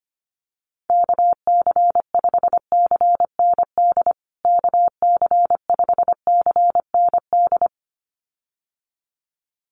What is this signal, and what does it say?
Morse code, heard as a keyed tone, about 25 wpm, 700 Hz. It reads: KC5CND KC5CND
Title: USA Call Signs Repeated 2x 25wpm